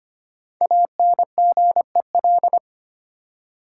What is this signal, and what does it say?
Morse code, heard as a keyed tone, about 25 wpm, 700 Hz. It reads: ANGEL
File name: 5 Letter Words - First 3 Letters Given - 25wpm